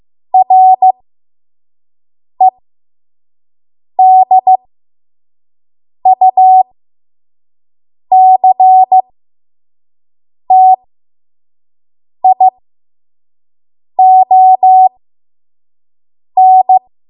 Morse code, heard as a keyed tone, about 15 wpm, spaced as for 5 wpm, 750 Hz. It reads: REDUCTION
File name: Morse / 05WPM